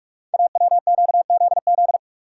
Morse code, heard as a keyed tone, about 44 wpm, 700 Hz. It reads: AWQZZ
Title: Code Groups - 5 Letters - 45wpm